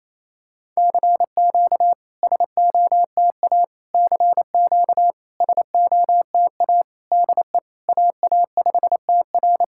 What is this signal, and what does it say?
Morse code, heard as a keyed tone, about 29 wpm, 700 Hz. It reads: CQ SOTA CQ SOTA DE AA5TR
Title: SOTA QSO Practice 28wpm